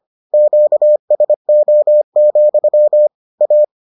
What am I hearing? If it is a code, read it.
Morse code, heard as a keyed tone, about 25 wpm, 600 Hz. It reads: QSO, A